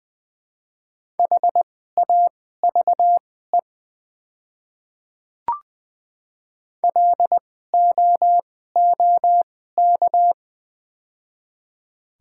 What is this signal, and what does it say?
Morse code, heard as a keyed tone, about 20 wpm, spaced as for 14 wpm, 700 Hz. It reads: HAVE LOOK
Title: 2x Character Spacing - Top 100 Words - 20wpm